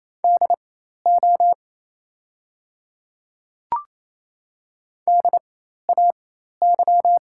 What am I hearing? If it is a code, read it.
Morse code, heard as a keyed tone, about 27 wpm, spaced as for 13 wpm, 690 Hz. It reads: DO DAY
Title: 4x Character Spacing - Top 200 Words - 28wpm